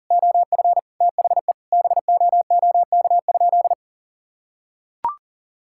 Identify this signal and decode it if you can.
Morse code, heard as a keyed tone, about 37 wpm, 700 Hz. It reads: OF THE BOOK?
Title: Sentences from Top 1000 Words - Review Part 5 - 40wpm